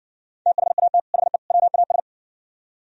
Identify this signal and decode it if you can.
Morse code, heard as a keyed tone, about 68 wpm, 700 Hz. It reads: THAT HE WAS